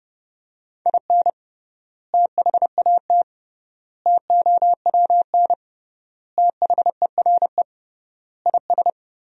Morse code, heard as a keyed tone, about 32 wpm, 700 Hz. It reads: IN THAT TOWN THERE IS
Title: 3x Word-Spacing - Sentences from Top 500 Words - 30wpm